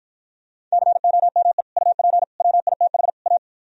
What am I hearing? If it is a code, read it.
Morse code, heard as a keyed tone, about 53 wpm, 700 Hz. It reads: COME UP WITH A